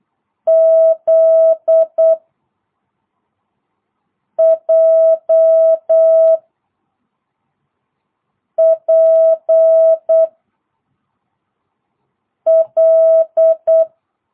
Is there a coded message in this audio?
Yes: Morse code from 0.5 to 13.8 s, heard as a keyed tone, about 8 wpm, 650 Hz. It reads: Z J P L